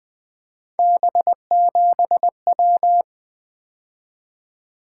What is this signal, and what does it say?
Morse code, heard as a keyed tone, about 20 wpm, 700 Hz. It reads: B7W
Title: Three Letter-Number Combinations - 20wpm